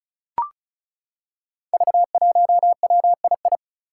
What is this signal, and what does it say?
Morse code, heard as a keyed tone, about 36 wpm, 700 Hz. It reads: V1WII
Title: Five Challenging Letter-Number Combinations 35wpm